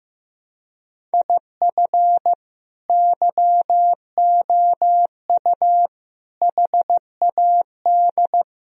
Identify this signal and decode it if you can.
Morse code, heard as a keyed tone, about 15 wpm, 700 Hz. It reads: IF YOU HAD
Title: Sentences from Top 1000 Words - Review Part 1 - 15wpm